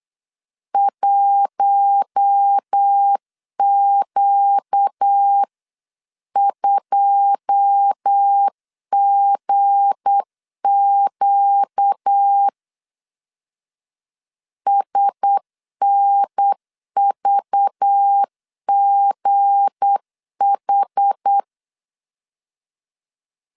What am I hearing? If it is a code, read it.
Morse code, heard as a keyed tone, about 8 wpm, 800 Hz. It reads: 1Q2GQ SNVGH
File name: CW / DET / 1 9 1500C S C 8wpm 5653